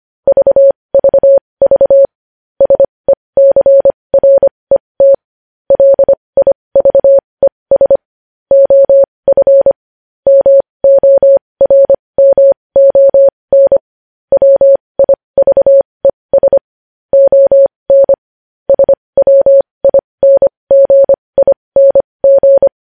Morse code, heard as a keyed tone, about 25 wpm, spaced as for 20 wpm, 550 Hz. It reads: VVV SECRET LIVES OF MORMON WIVES ON SWINGING
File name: podcast / News headlines-1x-25x20